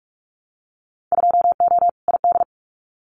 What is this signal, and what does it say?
Morse code, heard as a keyed tone, about 44 wpm, 700 Hz. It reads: 2O ID
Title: Winter Field Day Exchanges 45wpm